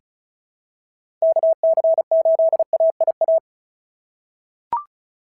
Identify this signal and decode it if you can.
Morse code, heard as a keyed tone, about 36 wpm, 690 Hz. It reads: KC8AIA
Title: SOTA QSO Practice 35wpm